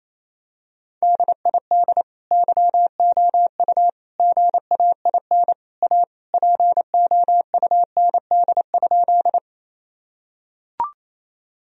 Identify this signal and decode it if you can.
Morse code, heard as a keyed tone, about 27 wpm, 700 Hz. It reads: DID YOU GAIN A POUND?
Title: Sentences from Top 1000 Words - Review Part 2 - 28wpm